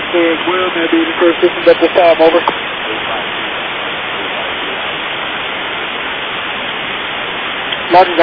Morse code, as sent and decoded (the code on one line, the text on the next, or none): none